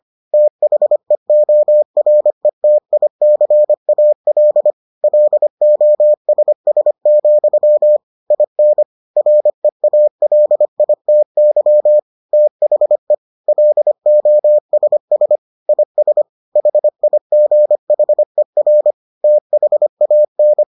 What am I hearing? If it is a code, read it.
Morse code, heard as a keyed tone, about 25 wpm, 600 Hz. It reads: THEORETICAL LOSS, IN REALITY THE LOSS IS HIGHER THAN